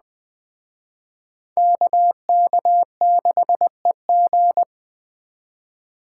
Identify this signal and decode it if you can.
Morse code, heard as a keyed tone, about 20 wpm, 700 Hz. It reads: KK6EG
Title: Speed Racing - USA 2x2 Call Signs 17wpm